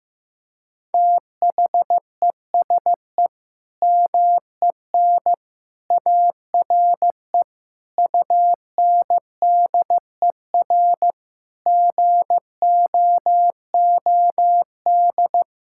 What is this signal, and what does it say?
Morse code, heard as a keyed tone, about 15 wpm, 700 Hz. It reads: THESE MEN ARE UNDER GOOD
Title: Sentences from Top 1000 Words - Review Part 6 - 15wpm